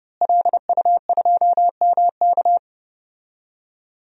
Morse code, heard as a keyed tone, about 30 wpm, 700 Hz. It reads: LU2MK
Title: DXCC-100 Call Signs - 5 Characters - Repeated 3x - 30wpm